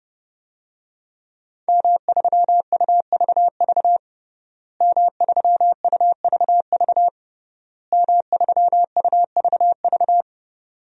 Morse code, heard as a keyed tone, about 30 wpm, 700 Hz. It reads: M3UVV M3UVV M3UVV
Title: DXCC-100 Call Signs - 5 Characters - Repeated 3x - 30wpm